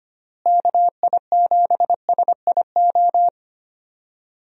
Morse code, heard as a keyed tone, about 25 wpm, 700 Hz. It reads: KI7SIO
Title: USA Call Signs - 6 Characters - Repeated 2x - 25wpm